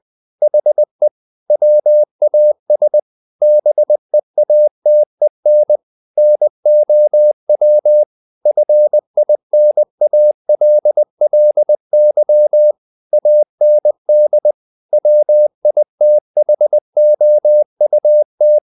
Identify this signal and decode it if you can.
Morse code, heard as a keyed tone, about 20 wpm, 600 Hz. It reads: HE WAS BEATEN NOW FINALLY AND WITHOUT
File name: the old man and the sea part 2 20wpm